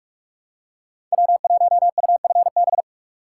Morse code, heard as a keyed tone, about 46 wpm, 700 Hz. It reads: W1UFB
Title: USA Call Signs - 5 Characters - 3x Character Spacing - 45wpm